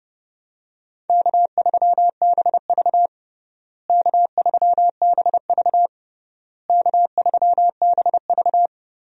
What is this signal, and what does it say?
Morse code, heard as a keyed tone, about 30 wpm, 700 Hz. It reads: K3BV K3BV K3BV
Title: USA Call Signs - 4 Characters - Repeated 3x - 30wpm